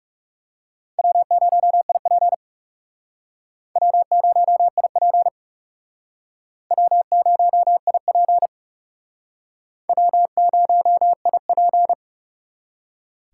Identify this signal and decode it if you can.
Morse code, heard as a keyed tone, about 37 wpm, 700 Hz. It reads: W0IP W0IP W0IP W0IP
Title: Speed Racing - USA 1x2 Call Signs 30wpm